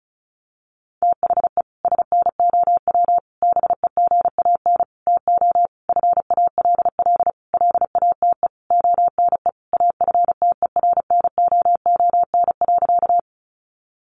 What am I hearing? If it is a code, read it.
Morse code, heard as a keyed tone, about 34 wpm, 700 Hz. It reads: THE SNOW BEGAN TO FALL LATE ONE AFTERNOON.
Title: Sentences from Top 1500 Words 35wpm